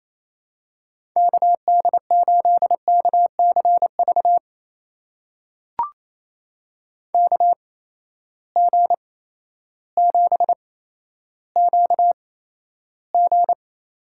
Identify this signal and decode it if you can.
Morse code, heard as a keyed tone, about 28 wpm, spaced as for 13 wpm, 700 Hz. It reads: KD8KCV KG7QG
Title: USA Call Signs - 6 Characters - 8x Character Spacing - 28wpm